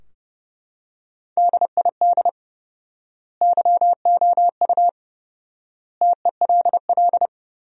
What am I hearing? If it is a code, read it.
Morse code, heard as a keyed tone, about 30 wpm, 700 Hz. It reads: DID YOU TELL
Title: 4x Word-Spacing - Sentences from Top 200 Words - 30wpm